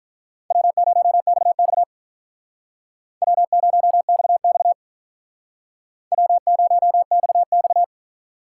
Morse code, heard as a keyed tone, about 45 wpm, 700 Hz. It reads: W0XX W0XX W0XX